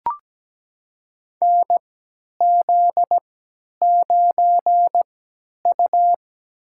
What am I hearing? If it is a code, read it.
Morse code, heard as a keyed tone, about 17 wpm, spaced as for 10 wpm, 700 Hz. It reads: NZ9U
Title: USA Call Signs - 3x Character Spacing - 17wpm